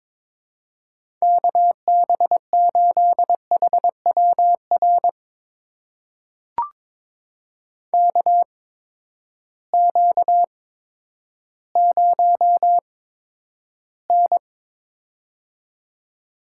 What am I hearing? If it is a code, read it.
Morse code, heard as a keyed tone, about 22 wpm, spaced as for 11 wpm, 700 Hz. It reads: KB8HWR KQ0N
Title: USA Call Signs - 8x Character Spacing - 22wpm